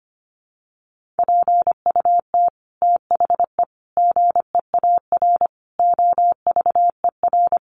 Morse code, heard as a keyed tone, about 25 wpm, 700 Hz. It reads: PUT THE GEAR OVER
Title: Sentences from Top 1400 Words 25wpm